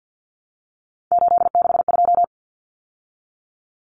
Morse code, heard as a keyed tone, about 50 wpm, 700 Hz. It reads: 862
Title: Common Cut Numbers - 3-Digit 50wpm